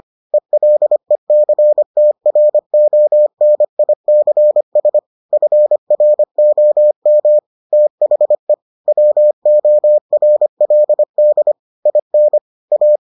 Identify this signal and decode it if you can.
Morse code, heard as a keyed tone, about 25 wpm, 600 Hz. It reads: ELECTRONICS FROM THE WORLD IN A